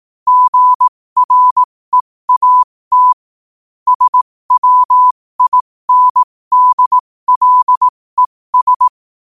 Morse code, heard as a keyed tone, about 18 wpm, spaced as for 15 wpm, 1000 Hz. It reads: GREAT SWINDLES